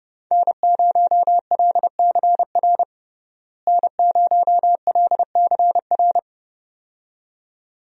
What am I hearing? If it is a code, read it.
Morse code, heard as a keyed tone, about 30 wpm, 700 Hz. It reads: N0LCR N0LCR
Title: USA Call Signs - 5 Characters - Repeated 2x - 30wpm